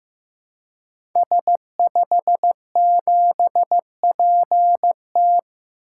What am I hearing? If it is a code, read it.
Morse code, heard as a keyed tone, about 15 wpm, 700 Hz. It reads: S57PT
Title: Code Groups - 5 Letter:Numbers - 15wpm